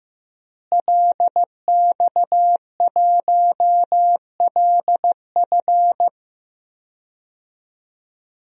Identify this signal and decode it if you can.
Morse code, heard as a keyed tone, about 16 wpm, 700 Hz. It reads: LX1LF
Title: DXCC-100 Call Signs - 5 Characters - Repeated 3x - 15wpm